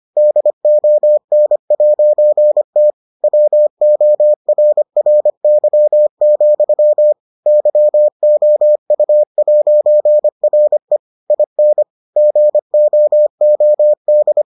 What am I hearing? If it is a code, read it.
Morse code, heard as a keyed tone, about 25 wpm, 600 Hz. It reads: DON'T WORRY, YOU'RE IN GOOD